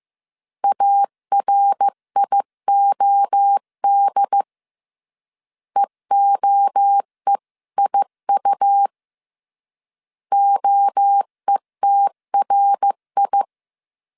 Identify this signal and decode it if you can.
Morse code, heard as a keyed tone, about 15 wpm, 800 Hz. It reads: ARIOD EOEIU OETRI
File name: CW / DET / 1 3 1500C S C 14wpm 2458